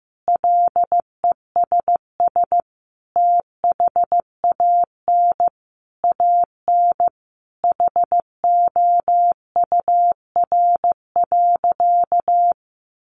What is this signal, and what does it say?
Morse code, heard as a keyed tone, about 15 wpm, 700 Hz. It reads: LESS THAN AN HOUR.
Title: Sentences from Top 500 Words 15wpm